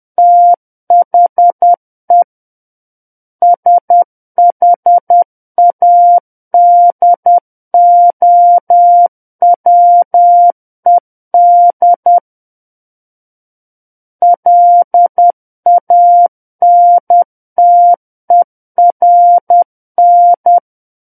Morse code, heard as a keyed tone, about 10 wpm, 700 Hz. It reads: THE SHADOWED LANTERN
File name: Book One / 01-Chapter 1 - 10 WPM